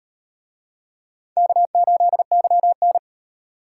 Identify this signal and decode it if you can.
Morse code, heard as a keyed tone, about 33 wpm, 700 Hz. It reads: K8YN